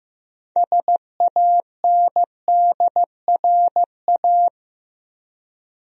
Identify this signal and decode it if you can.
Morse code, heard as a keyed tone, about 15 wpm, 700 Hz. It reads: SANDRA
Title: USA Common Names - 15wpm